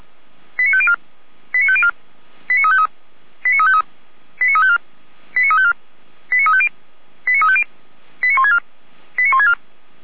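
Five-tone selective call (ZVEI), heard as quick runs of five tones, tones of about 70 ms. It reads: 89504 89504 89353 89353 89354 89354 89350 89350 89164 89164
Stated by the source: ZVEI five-tone selective call received on SDR, fire brigade Feuerwehr Orlamünde.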